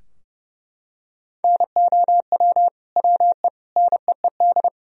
Morse code, heard as a keyed tone, about 30 wpm, 700 Hz. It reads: NOW WE NEED
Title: Sentences from Top 1000 Words - Review Part 4 - 30wpm